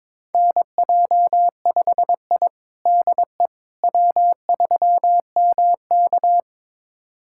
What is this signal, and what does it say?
Morse code, heard as a keyed tone, about 22 wpm, 700 Hz. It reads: NJ5I DE W3MK
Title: USA Callsign DE Callsign - 4 Characters - 22wpm